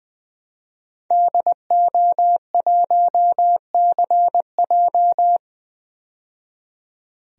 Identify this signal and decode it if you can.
Morse code, heard as a keyed tone, about 20 wpm, 700 Hz. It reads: DO1CJ
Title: DXCC-100 Call Signs - 5 Characters - 20wpm